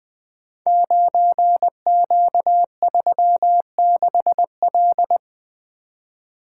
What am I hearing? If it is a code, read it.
Morse code, heard as a keyed tone, about 20 wpm, 700 Hz. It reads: 9Q36L
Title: Code Groups - 5 Letter:Numbers - 20wpm